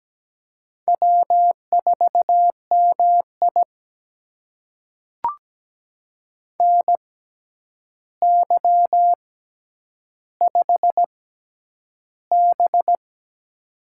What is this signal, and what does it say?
Morse code, heard as a keyed tone, about 17 wpm, spaced as for 9 wpm, 700 Hz. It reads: W4MI NY5B
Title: USA Call Signs - 4 Characters - 6x Character Spacing - 17wpm